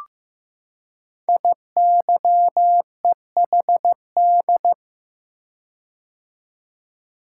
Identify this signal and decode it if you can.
Morse code, heard as a keyed tone, about 15 wpm, 700 Hz. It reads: IYEHD
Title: Code Groups - 5 Letters - 15wpm